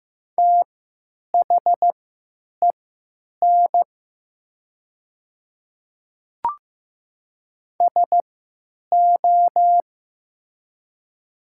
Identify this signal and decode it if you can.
Morse code, heard as a keyed tone, about 15 wpm, spaced as for 9 wpm, 700 Hz. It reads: THEN SO
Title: 3x Character Spacing - Top 100 Words - 15wpm